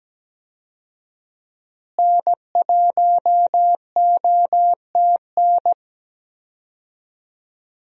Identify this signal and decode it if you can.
Morse code, heard as a keyed tone, about 17 wpm, 700 Hz. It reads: N1OTN
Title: USA Call Signs - 5 Characters - Repeated 2x - 17wpm